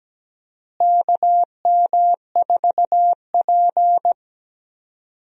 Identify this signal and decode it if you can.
Morse code, heard as a keyed tone, about 17 wpm, 700 Hz. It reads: KM4P
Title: USA Call Signs - 4 Characters - Repeated 3x - 17wpm